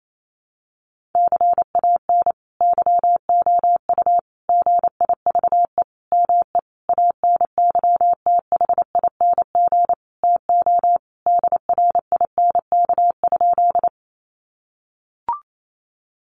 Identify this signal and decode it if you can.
Morse code, heard as a keyed tone, about 28 wpm, 700 Hz. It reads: CAN YOU GIVE ME ANYTHING TO DRINK?
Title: Sentences from Top 1500 Words 28wpm